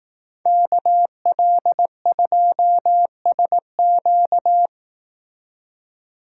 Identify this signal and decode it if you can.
Morse code, heard as a keyed tone, about 18 wpm, 700 Hz. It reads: KL2SQ
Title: Speed Racing - USA 2x2 Call Signs 15wpm